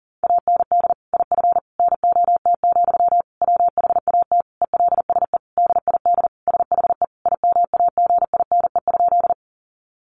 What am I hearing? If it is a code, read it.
Morse code, heard as a keyed tone, about 37 wpm, 700 Hz. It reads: AND IF NOT, WHAT ELSE DID SHE IMAGINE?
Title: Sentences from Top 1400 Words 40wpm